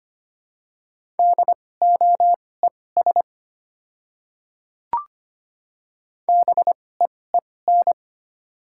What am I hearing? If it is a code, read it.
Morse code, heard as a keyed tone, about 25 wpm, spaced as for 18 wpm, 700 Hz. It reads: DOES BEEN
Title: 2x Character Spacing - Top 300 Words - 25wpm